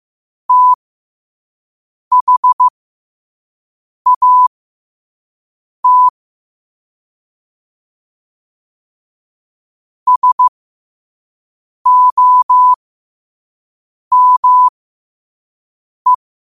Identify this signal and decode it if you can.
Morse code, heard as a keyed tone, about 15 wpm, spaced as for 5 wpm, 1000 Hz. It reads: THAT SOME